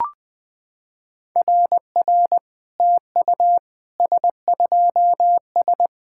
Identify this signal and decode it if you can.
Morse code, heard as a keyed tone, about 20 wpm, 690 Hz. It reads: RR TU S2S
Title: SOTA QSO Practice 20wpm